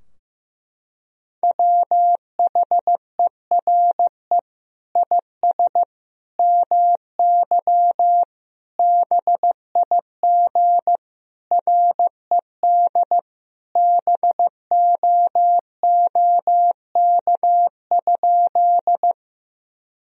Morse code, heard as a keyed tone, about 15 wpm, 700 Hz. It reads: WHERE IS MY BIG RED BOOK?
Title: Sentences from Top 1000 Words - Review Part 2 - 15wpm